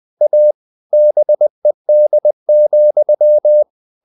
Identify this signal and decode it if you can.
Morse code, heard as a keyed tone, about 20 wpm, 600 Hz. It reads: A BED,
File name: the old man and the sea / the old man and the sea part 1 20wpm